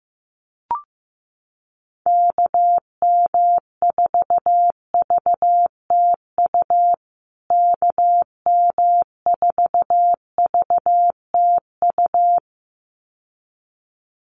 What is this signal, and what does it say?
Morse code, heard as a keyed tone, about 15 wpm, 700 Hz. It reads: KM4VTU KM4VTU
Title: USA Call Signs Repeated 2x 15wpm